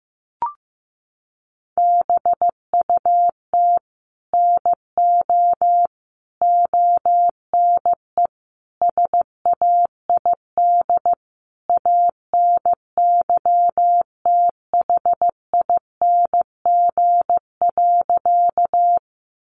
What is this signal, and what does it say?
Morse code, heard as a keyed tone, about 15 wpm, 700 Hz. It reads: BUT NO ONE SAID ANYTHING.
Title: Sentences from Top 1000 Words 15wpm